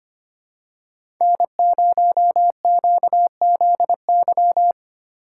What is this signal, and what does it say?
Morse code, heard as a keyed tone, about 25 wpm, 700 Hz. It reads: N0QZY
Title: USA Call Signs - 5 Characters - 2x Character Spacing - 25wpm